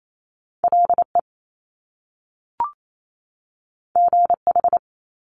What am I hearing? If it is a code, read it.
Morse code, heard as a keyed tone, about 28 wpm, 700 Hz. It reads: LE GH
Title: Top Two Letter Combinations in Words 28wpm